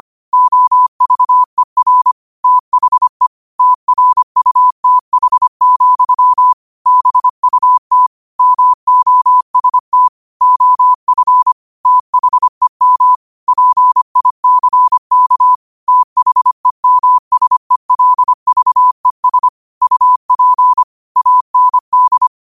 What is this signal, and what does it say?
Morse code, heard as a keyed tone, about 25 wpm, 1000 Hz. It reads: OVER THE TRUTH, BUT MOST OF THEM PICK THEMSELVES UP AND
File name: qotdfiles / 25wpm